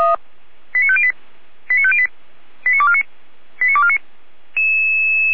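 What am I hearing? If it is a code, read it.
Five-tone selective call (ZVEI), heard as quick runs of five tones, tones of about 70 ms. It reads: 89508 89508 89249 89249
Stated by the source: ZVEI five-tone call received by SDR, from the Feuerwehr Orlamünde fire brigade archive.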